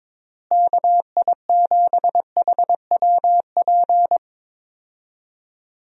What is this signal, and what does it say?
Morse code, heard as a keyed tone, about 22 wpm, 700 Hz. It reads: KI7HWP
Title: USA Call Signs - 6 Characters - Repeated 3x - 22wpm